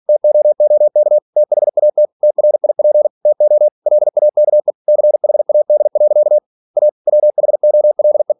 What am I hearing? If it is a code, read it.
Morse code, heard as a keyed tone, about 46 wpm, 600 Hz. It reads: TOOK THAT TRIP TO LAKE CHAD. A WHOLE
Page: Never 47wpm